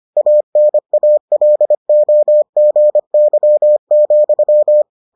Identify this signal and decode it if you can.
Morse code, heard as a keyed tone, about 25 wpm, 600 Hz. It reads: ANALOGY,